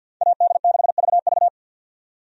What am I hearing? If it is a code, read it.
Morse code, heard as a keyed tone, about 50 wpm, 700 Hz. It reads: AD6VV
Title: USA Call Signs - 5 Characters - 50wpm